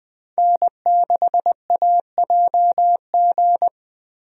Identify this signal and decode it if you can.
Morse code, heard as a keyed tone, about 20 wpm, 700 Hz. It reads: N6AJG
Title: USA Call Signs - 5 Characters - Repeated 2x - 20wpm